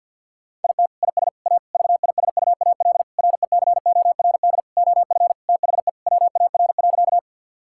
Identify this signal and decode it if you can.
Morse code, heard as a keyed tone, about 52 wpm, 700 Hz. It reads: IT IS A VISUAL RECORD OF THE WAR.
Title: Sentences from Top 1000 Words - Review Part 6 - 50wpm